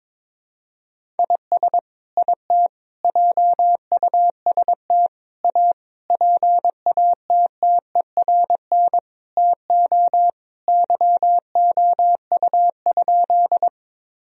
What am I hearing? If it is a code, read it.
Morse code, heard as a keyed tone, about 22 wpm, 700 Hz. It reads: IS IT JUST A PATTERN TO YOU?
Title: Sentences from Top 1000 Words - Review Part 2 - 22wpm